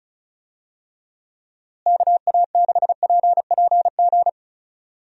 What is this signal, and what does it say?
Morse code, heard as a keyed tone, about 36 wpm, 700 Hz. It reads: KA6PPG
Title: USA Call Signs - 6 Characters - Repeated 2x - 35wpm